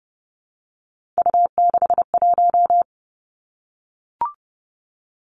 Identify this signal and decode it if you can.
Morse code, heard as a keyed tone, about 30 wpm, 700 Hz. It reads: U61
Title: Three Challenging Letter-Number Combinations 30wpm